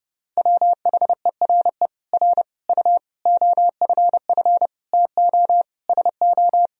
Morse code, heard as a keyed tone, about 30 wpm, 700 Hz. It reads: WHERE R U OFF TO SO
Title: Sentences from Top 400 Words and Abbreviations 30wpm